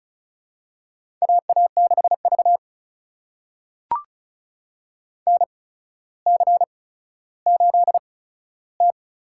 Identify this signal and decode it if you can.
Morse code, heard as a keyed tone, about 36 wpm, spaced as for 15 wpm, 700 Hz. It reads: AA6V NC8T
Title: USA Call Signs - 4 Characters - 8x Character Spacing - 35wpm